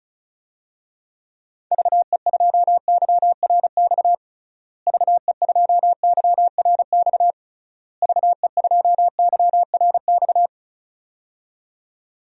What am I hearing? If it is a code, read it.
Morse code, heard as a keyed tone, about 36 wpm, 700 Hz. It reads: VE2YRX VE2YRX VE2YRX
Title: DXCC-100 Call Signs - 6 Characters - Repeated 3x - 35wpm